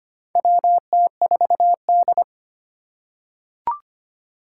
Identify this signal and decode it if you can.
Morse code, heard as a keyed tone, about 25 wpm, 700 Hz. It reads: WT4D
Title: USA Call Signs - 4 Characters - 3x Character Spacing - 25wpm